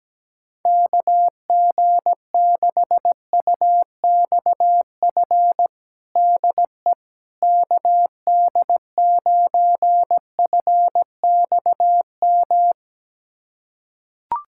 Morse code, heard as a keyed tone, about 17 wpm, 700 Hz. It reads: KG6UXF DE KD9FXM
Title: USA Callsign DE Callsign 17wpm